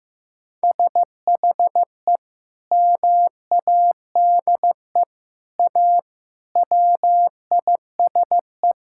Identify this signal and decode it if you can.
Morse code, heard as a keyed tone, about 15 wpm, 700 Hz. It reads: SHE MADE A WISE
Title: Sentences from Top 1600 Words 15wpm